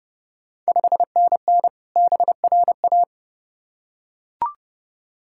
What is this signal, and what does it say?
Morse code, heard as a keyed tone, about 32 wpm, 700 Hz. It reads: 5NN BRA
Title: Ontario QSO Party Exchange 30wpm